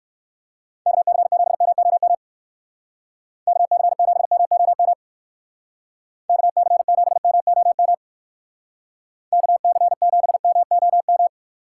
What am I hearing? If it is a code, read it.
Morse code, heard as a keyed tone, about 57 wpm, 700 Hz. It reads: KC7MOM KC7MOM KC7MOM KC7MOM